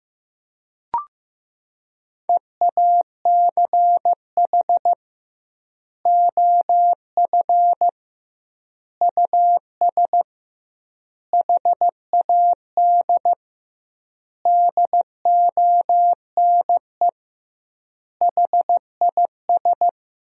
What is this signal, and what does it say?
Morse code, heard as a keyed tone, about 15 wpm, 700 Hz. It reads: EACH OF US HAD DONE HIS
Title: 2x Word-Spacing - Sentences from Top 500 Words - 15wpm